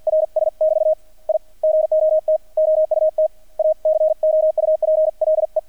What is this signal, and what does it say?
Morse code, heard as a keyed tone, about 50 wpm, 630 Hz. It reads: RS <BT> I GOT OUT A COUPLE
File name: archived / CRSnet